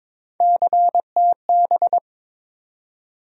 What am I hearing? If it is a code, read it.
Morse code, heard as a keyed tone, about 21 wpm, 700 Hz. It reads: CTB
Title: USA Call Sign Postfixes - 3 Letters 22wpm